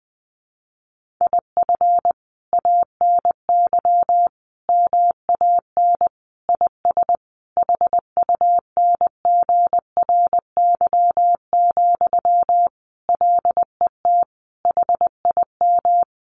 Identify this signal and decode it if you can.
Morse code, heard as a keyed tone, about 19 wpm, 700 Hz. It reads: IF ANY MAN IS HUNGRY, LET HIM
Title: Sentences from Top 1400 Words 20wpm